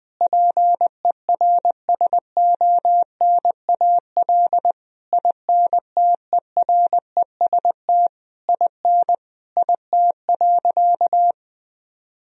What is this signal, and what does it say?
Morse code, heard as a keyed tone, about 20 wpm, 700 Hz. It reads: PERSONAL INTEREST IN IT.
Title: Sentences from Top 1000 Words - Review Part 4 - 20wpm